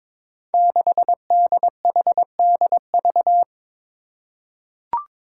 Morse code, heard as a keyed tone, about 22 wpm, 700 Hz. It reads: 6DHDV
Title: Five Challenging Letter-Number Combinations 22wpm